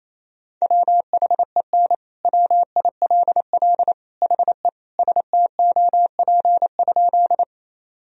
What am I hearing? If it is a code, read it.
Morse code, heard as a keyed tone, about 28 wpm, 700 Hz. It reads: WHEN WILL HE STOP?
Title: Sentences from Top 1000 Words - Review Part 1 - 28wpm